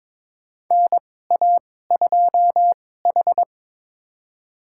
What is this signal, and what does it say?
Morse code, heard as a keyed tone, about 22 wpm, spaced as for 16 wpm, 700 Hz. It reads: NA2H
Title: USA Call Signs - 4 Characters - 2x Character Spacing - 22wpm